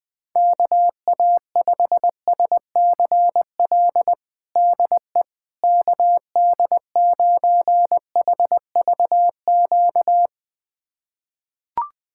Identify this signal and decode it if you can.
Morse code, heard as a keyed tone, about 20 wpm, 700 Hz. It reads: KA5SCL DE KD9HVQ
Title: USA Callsign DE Callsign - 6 Characters - 20wpm